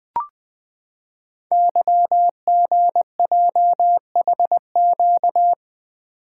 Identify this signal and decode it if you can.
Morse code, heard as a keyed tone, about 20 wpm, 700 Hz. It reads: YGJHQ
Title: Code Groups - 5 Letters - 20wpm